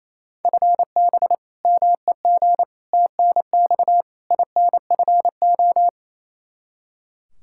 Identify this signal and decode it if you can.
Morse code, heard as a keyed tone, about 28 wpm, 700 Hz. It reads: FB MEG TNX INFO
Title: Practice QSO - Exchange 3 - More Info - 28wpm